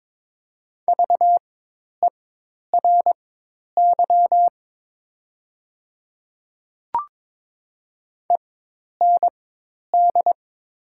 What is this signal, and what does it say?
Morse code, heard as a keyed tone, about 22 wpm, spaced as for 10 wpm, 700 Hz. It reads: VERY END
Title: 4x Character Spacing - Top 200 Words - 22wpm